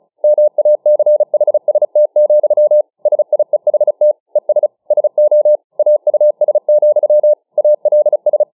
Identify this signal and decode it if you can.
Morse code, heard as a keyed tone, about 34 wpm, 600 Hz. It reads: MACHST, SIEHT ES SO AUS, ALS